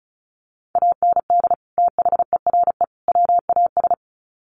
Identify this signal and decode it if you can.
Morse code, heard as a keyed tone, about 34 wpm, 700 Hz. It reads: AND THERE WAS
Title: Sets of 3 Words 35wpm